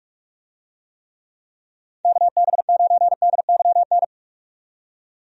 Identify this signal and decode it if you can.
Morse code, heard as a keyed tone, about 46 wpm, 700 Hz. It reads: KB9DYN
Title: USA Call Signs - 6 Characters - 45wpm